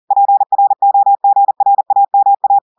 Morse code, heard as a keyed tone, about 40 wpm, 800 Hz. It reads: PROGRAMA
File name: audios / portugais / portugais texte en clair 2 vitesse 40 mots minutes